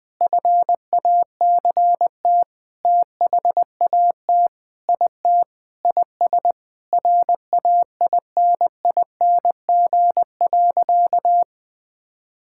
Morse code, heard as a keyed tone, about 20 wpm, 700 Hz. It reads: FACT THAT IT IS RAINING.
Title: Sentences with 8-Word Phrases - 20wpm